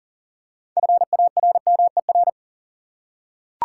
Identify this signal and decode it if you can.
Morse code, heard as a keyed tone, about 40 wpm, 700 Hz. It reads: FARMER
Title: Six Letter Words - 40wpm